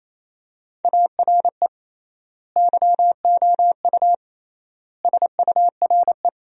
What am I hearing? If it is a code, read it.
Morse code, heard as a keyed tone, about 27 wpm, 700 Hz. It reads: ARE YOU SURE
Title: 3x Word-Spacing - Sentences from Top 300 Words - 28wpm